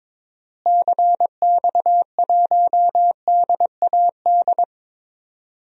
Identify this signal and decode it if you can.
Morse code, heard as a keyed tone, about 22 wpm, 700 Hz. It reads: CX1DAD
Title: DXCC-100 Call Signs - 6 Characters - Repeated 2x - 22wpm